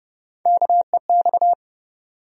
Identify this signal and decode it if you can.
Morse code, heard as a keyed tone, about 30 wpm, 700 Hz. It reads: KEX
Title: USA Call Sign Postfixes - 3 Letters 30wpm